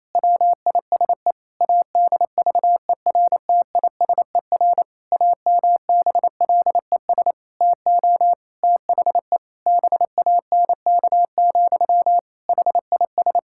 Morse code, heard as a keyed tone, about 28 wpm, 700 Hz. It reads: WISE ADVERTISER AMBLES TO THE BANK, HIS
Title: Pangram Challenge v2 - 28wpm